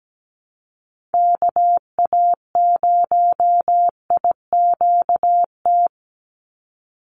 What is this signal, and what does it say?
Morse code, heard as a keyed tone, about 17 wpm, 700 Hz. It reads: KA0IQT
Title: Call Signs - 17wpm